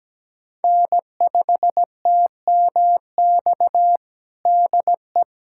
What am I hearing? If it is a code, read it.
Morse code, heard as a keyed tone, about 17 wpm, 700 Hz. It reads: N5TMX DE
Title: USA Callsign DE Callsign - 5 Characters - 17wpm